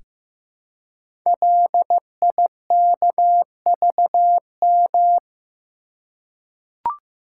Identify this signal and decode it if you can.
Morse code, heard as a keyed tone, about 15 wpm, 700 Hz. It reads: LIKVM
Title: Code Groups - 5 Letters - 15wpm